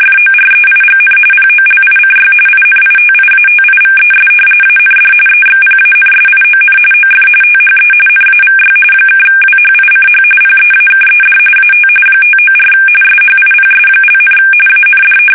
RTTY, about 45.45 baud. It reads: FOREIGN POLICY INITIATIVES AND EXPANSION OF THE NAVY (WHICH PEAKED AT 594 SHIPS), AND
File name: fsk850 / battleshipnj